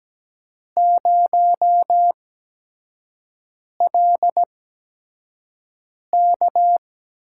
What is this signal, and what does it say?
Morse code, heard as a keyed tone, about 17 wpm, spaced as for 5 wpm, 700 Hz. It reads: 0LK
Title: USA Call Signs - 4 Characters - 8x Character Spacing - 17wpm